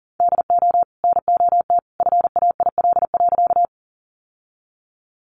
Morse code, heard as a keyed tone, about 40 wpm, 700 Hz. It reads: DO NOT FAIL.
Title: Sentences from Top 1100 Words 40wpm